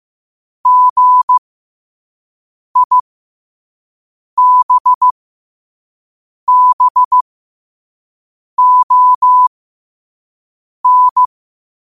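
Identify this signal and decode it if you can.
Morse code, heard as a keyed tone, about 15 wpm, spaced as for 5 wpm, 1000 Hz. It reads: GIBBON
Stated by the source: Morse code quote of the day sent at 5 wpm.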